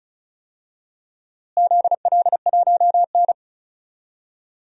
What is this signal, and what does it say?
Morse code, heard as a keyed tone, about 36 wpm, 700 Hz. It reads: ZL1N
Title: DXCC-100 Call Signs - 4 Characters - Repeated 3x - 35wpm